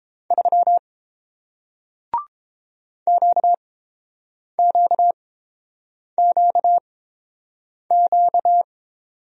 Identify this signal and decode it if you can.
Morse code, heard as a keyed tone, about 30 wpm, spaced as for 7 wpm, 700 Hz. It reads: 3 QQQQ